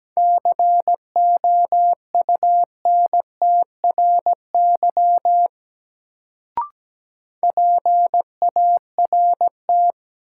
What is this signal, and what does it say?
Morse code, heard as a keyed tone, about 17 wpm, 700 Hz. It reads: COUNTRY PART
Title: Sets of 2 Words - New Word Drill - 17wpm